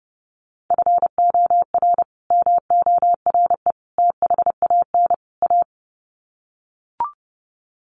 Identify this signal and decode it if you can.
Morse code, heard as a keyed tone, about 30 wpm, 700 Hz. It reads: FOR MORE THAN A
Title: Sets of 4 Words 30wpm